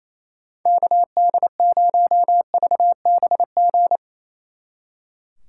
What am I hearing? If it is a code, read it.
Morse code, heard as a keyed tone, about 36 wpm, 700 Hz. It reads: KD0VBG